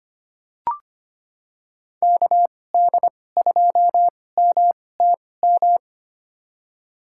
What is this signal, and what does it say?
Morse code, heard as a keyed tone, about 25 wpm, spaced as for 18 wpm, 700 Hz. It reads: KD2MTM
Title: USA Call Signs - 2x Character Spacing - 25wpm